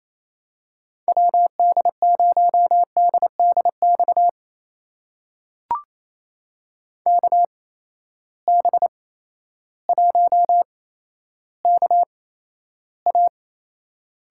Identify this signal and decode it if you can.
Morse code, heard as a keyed tone, about 27 wpm, spaced as for 13 wpm, 700 Hz. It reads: WD0DDX KB1KA
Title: USA Call Signs - 6 Characters - 8x Character Spacing - 28wpm